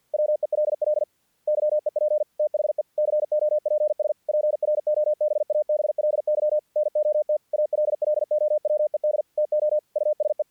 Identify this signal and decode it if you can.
Morse code, heard as a keyed tone, about 50 wpm, 610 Hz. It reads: WELL YEP THE COPS PROBABLY NOT ALLOWED TO USE
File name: archived / CRSnet